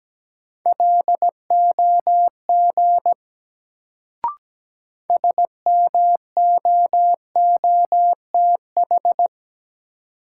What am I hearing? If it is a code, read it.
Morse code, heard as a keyed tone, about 17 wpm, 700 Hz. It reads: LOG SMOOTH
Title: Sentences from Top 1400 Words - New Word Drill Alt - 17wpm